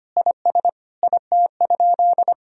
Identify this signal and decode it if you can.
Morse code, heard as a keyed tone, about 25 wpm, 700 Hz. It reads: IS IT?
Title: Sentences from Top 1000 Words - Review Part 4 - 25wpm